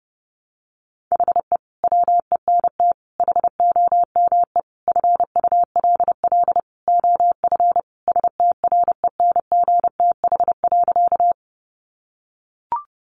Morse code, heard as a keyed tone, about 30 wpm, 700 Hz. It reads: HE WENT HOME FULL OF STRENGTH.
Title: Sentences from Top 1300 Words 30wpm